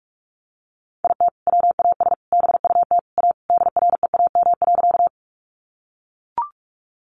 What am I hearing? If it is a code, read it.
Morse code, heard as a keyed tone, about 44 wpm, 700 Hz. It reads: IT WAS BUT A DREAM.
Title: Sentences from Top 1500 Words 45wpm